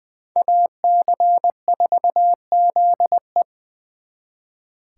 Morse code, heard as a keyed tone, about 24 wpm, 700 Hz. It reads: AC4ZE
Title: Speed Racing - USA Call Signs 20wpm